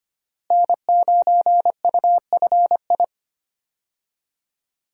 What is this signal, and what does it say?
Morse code, heard as a keyed tone, about 25 wpm, 700 Hz. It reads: N9UFI
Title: USA Call Signs - 5 Characters - Repeated 3x - 25wpm